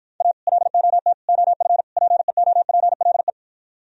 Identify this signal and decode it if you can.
Morse code, heard as a keyed tone, about 53 wpm, 700 Hz. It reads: A LOT OF PEOPLE